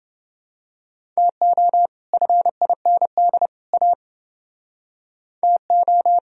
Morse code, heard as a keyed tone, about 29 wpm, 700 Hz. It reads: TO FIND A TO